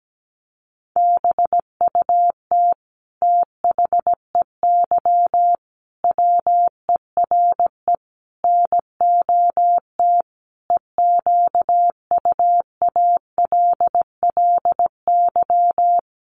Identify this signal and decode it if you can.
Morse code, heard as a keyed tone, about 17 wpm, 700 Hz. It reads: BUT THEY WERE NOT EQUALLY
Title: Sentences from Top 1400 Words 17wpm